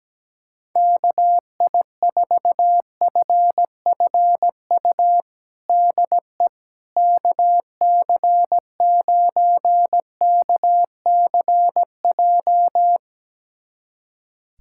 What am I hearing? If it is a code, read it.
Morse code, heard as a keyed tone, about 17 wpm, 700 Hz. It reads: KI4FFU DE KC9KCJ
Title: USA Callsign DE Callsign 17wpm